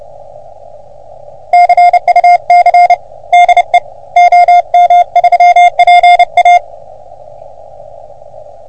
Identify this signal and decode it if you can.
Morse code, heard as a keyed tone, about 30 wpm, 720 Hz. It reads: CUC DE OM3PA
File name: OM3PA PILUP CUC 388